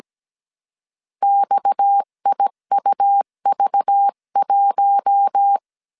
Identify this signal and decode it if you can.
Morse code, heard as a keyed tone, about 17 wpm, 800 Hz. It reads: XIUV1
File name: CW / DET / 1 6 1500C S C 16wpm 2733